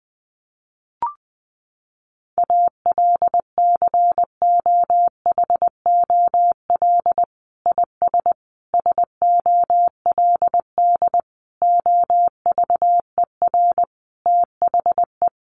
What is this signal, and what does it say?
Morse code, heard as a keyed tone, about 20 wpm, 700 Hz. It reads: ALCOHOL IS SOLD OVER THE
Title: Sentences from Top 1400 Words 20wpm